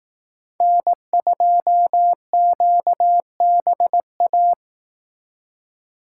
Morse code, heard as a keyed tone, about 18 wpm, 700 Hz. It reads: N2QBA
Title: Speed Racing - USA Call Signs 15wpm